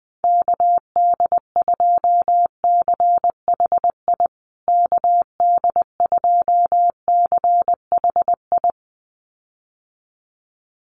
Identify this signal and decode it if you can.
Morse code, heard as a keyed tone, about 20 wpm, 700 Hz. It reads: KD2CHI KD2CHI
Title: USA Call Signs Repeated 2x 20wpm